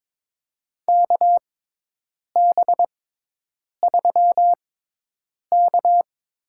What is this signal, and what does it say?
Morse code, heard as a keyed tone, about 22 wpm, spaced as for 8 wpm, 700 Hz. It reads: KB3K
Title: USA Call Signs - 6 Characters - 6x Character Spacing - 22wpm